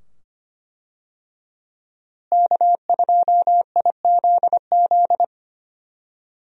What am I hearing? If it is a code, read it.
Morse code, heard as a keyed tone, about 25 wpm, 700 Hz. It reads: K2IZZ
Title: USA Call Signs - 5 Characters - Repeated 2x - 25wpm